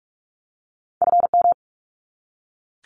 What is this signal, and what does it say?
Morse code, heard as a keyed tone, about 44 wpm, 700 Hz. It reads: FM
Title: Abbreviations 45wpm